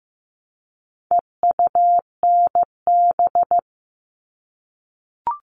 Morse code, heard as a keyed tone, about 15 wpm, 700 Hz. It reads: EUNB
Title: All Cut Numbers - 4-Digit 15wpm